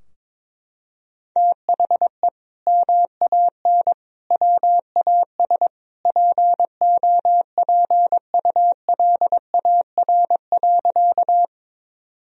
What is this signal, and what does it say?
Morse code, heard as a keyed tone, about 22 wpm, 700 Hz. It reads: THE MAN WAS POPULAR.
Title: Sentences from Top 1000 Words - Review Part 3 - 22wpm